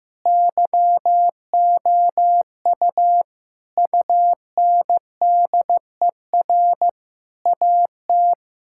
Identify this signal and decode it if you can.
Morse code, heard as a keyed tone, about 15 wpm, 700 Hz. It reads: YOU UNDER AT
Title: Sentences from Top 1000 Words - Review Part 6 - 15wpm